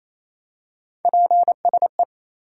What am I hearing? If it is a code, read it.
Morse code, heard as a keyed tone, about 28 wpm, 700 Hz. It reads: PSE
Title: Top 90 QSO Elements 28wpm